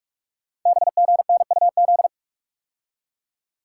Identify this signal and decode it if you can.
Morse code, heard as a keyed tone, about 44 wpm, 700 Hz. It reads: BGNUZ
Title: Code Groups - 5 Letters - 45wpm